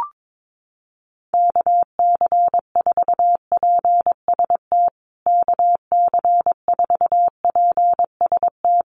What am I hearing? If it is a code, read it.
Morse code, heard as a keyed tone, about 22 wpm, 700 Hz. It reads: KC4PST KC4PST
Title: USA Call Signs Repeated 3x 22wpm